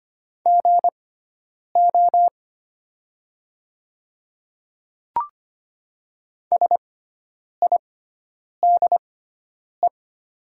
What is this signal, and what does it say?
Morse code, heard as a keyed tone, about 25 wpm, spaced as for 9 wpm, 700 Hz. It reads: GO SIDE
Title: 6x Character Spacing - Top 100 Words - 25wpm